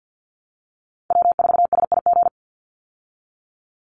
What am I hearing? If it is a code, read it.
Morse code, heard as a keyed tone, about 49 wpm, 700 Hz. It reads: W4SIG
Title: Call Signs Repeated 3x - 50wpm